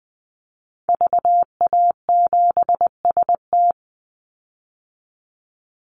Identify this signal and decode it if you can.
Morse code, heard as a keyed tone, about 20 wpm, 700 Hz. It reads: VA7ST
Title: Call Signs Repeated 3x - 20wpm